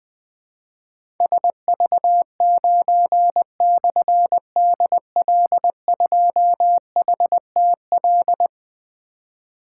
Morse code, heard as a keyed tone, about 20 wpm, 700 Hz. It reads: SV9/DL2HTL
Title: Difficult Call Signs 20wpm